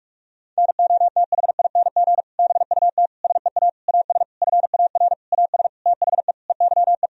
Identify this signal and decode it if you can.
Morse code, heard as a keyed tone, about 46 wpm, 700 Hz. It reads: NOTHING BUT SEA AS FAR AS THE EYE
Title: Sentences from Top 1000 Words - Review Part 1 - 45wpm